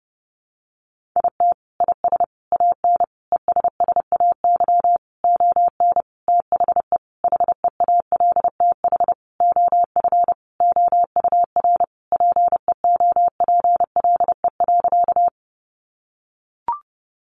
Morse code, heard as a keyed tone, about 30 wpm, 700 Hz. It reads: IT IS AN ESSAY ON THE HEALTH OF OUR PEOPLE.
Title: Sentences from Top 1500 Words 30wpm